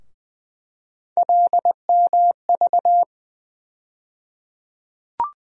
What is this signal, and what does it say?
Morse code, heard as a keyed tone, about 20 wpm, 700 Hz. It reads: LMV
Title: USA Call Sign Postfixes - 3 Letters 20wpm